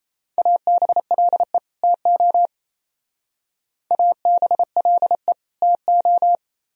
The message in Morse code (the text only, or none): ABLE TO ABLE TO